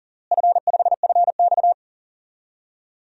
Morse code, heard as a keyed tone, about 37 wpm, 700 Hz. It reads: F5FX
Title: DXCC-100 Call Signs - 4 Characters - Repeated 3x - 40wpm